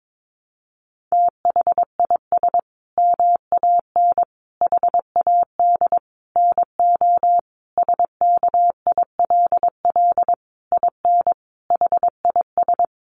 Morse code, heard as a keyed tone, about 22 wpm, 700 Hz. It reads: THIS MAN HAD NO SKILL IN HIS
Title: Sentences from Top 1200 Words 22wpm